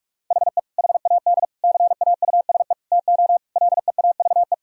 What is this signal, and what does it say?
Morse code, heard as a keyed tone, about 46 wpm, 700 Hz. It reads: HE HAD CAUSE TO LEAVE
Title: Sentences from Top 1000 Words - Review Part 1 - 45wpm